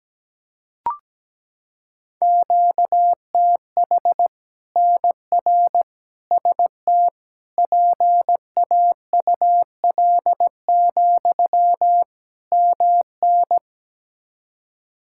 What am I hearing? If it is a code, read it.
Morse code, heard as a keyed tone, about 17 wpm, 700 Hz. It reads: QTH NR ST PAUL, MN
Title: K1USN SST QSO Practice 17wpm